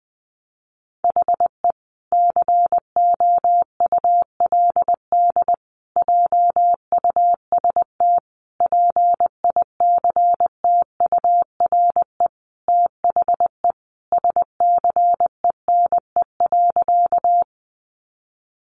Morse code, heard as a keyed tone, about 20 wpm, 700 Hz. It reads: HE COULD JUST PICTURE THE SCENE.
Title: Sentences from Top 1100 Words 20wpm